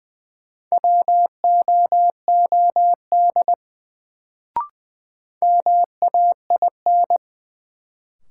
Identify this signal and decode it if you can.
Morse code, heard as a keyed tone, about 20 wpm, 700 Hz. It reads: WOOD MAIN
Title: Sentences from Top 1000 Words - Review Part 2 - New Word Drill - 20wpm